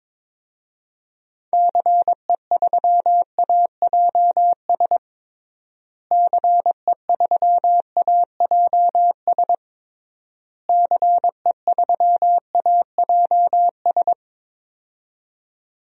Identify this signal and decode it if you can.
Morse code, heard as a keyed tone, about 22 wpm, 700 Hz. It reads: CE3AJS CE3AJS CE3AJS
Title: DXCC-100 Call Signs - 6 Characters - Repeated 3x - 22wpm